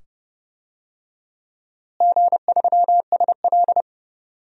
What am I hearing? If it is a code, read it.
Morse code, heard as a keyed tone, about 30 wpm, 700 Hz. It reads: G3SL
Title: DXCC-100 Call Signs - 4 Characters - Repeated 2x - 30wpm